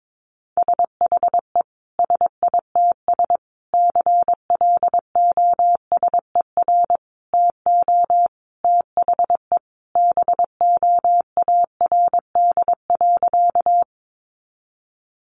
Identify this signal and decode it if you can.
Morse code, heard as a keyed tone, about 22 wpm, 700 Hz. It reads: SHE SITS CLOSER TO THE BOARD.
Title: Additional Easy Phrases 22wpm